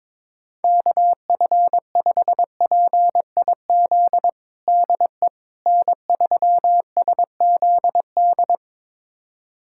Morse code, heard as a keyed tone, about 22 wpm, 700 Hz. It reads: KF5PIZ DE N3SZD
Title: USA Callsign DE Callsign 22wpm